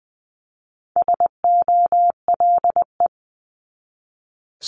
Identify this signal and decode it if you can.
Morse code, heard as a keyed tone, about 20 wpm, 700 Hz. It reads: SOLE
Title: 260 Lesson 42e - Words 20-20wpm